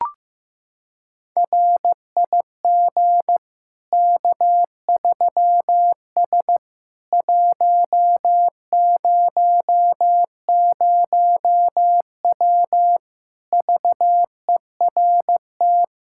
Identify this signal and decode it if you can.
Morse code, heard as a keyed tone, about 15 wpm, 700 Hz. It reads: RIG K3S 100W VERT
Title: K1USN SST QSO Practice 15wpm